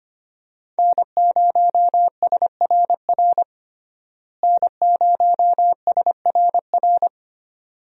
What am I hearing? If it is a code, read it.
Morse code, heard as a keyed tone, about 25 wpm, 700 Hz. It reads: N0SRR N0SRR
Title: USA Call Signs - 5 Characters - Repeated 2x - 25wpm